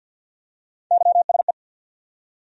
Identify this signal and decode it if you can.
Morse code, heard as a keyed tone, about 48 wpm, 700 Hz. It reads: YSE
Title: USA Call Sign Postfixes - 3 Letters 50wpm